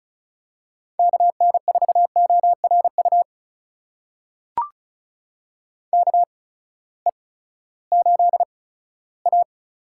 Morse code, heard as a keyed tone, about 36 wpm, spaced as for 18 wpm, 700 Hz. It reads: KN4ORU KE8A
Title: USA Call Signs - 6 Characters - 8x Character Spacing - 35wpm